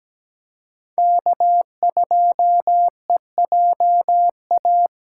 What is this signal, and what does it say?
Morse code, heard as a keyed tone, about 17 wpm, 700 Hz. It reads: K2EJA
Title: USA Call Signs - 5 Characters - 4x Character Spacing - 17wpm